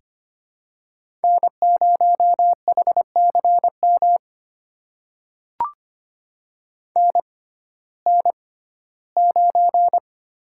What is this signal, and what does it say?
Morse code, heard as a keyed tone, about 25 wpm, spaced as for 15 wpm, 700 Hz. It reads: N0HCM NN9
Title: USA Call Signs - 6x Character Spacing - 25wpm